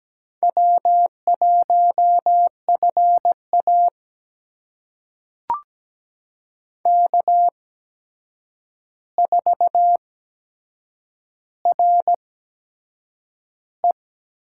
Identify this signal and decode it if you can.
Morse code, heard as a keyed tone, about 17 wpm, spaced as for 7 wpm, 700 Hz. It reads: W1FA K4RE
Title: USA Call Signs - 4 Characters - 8x Character Spacing - 17wpm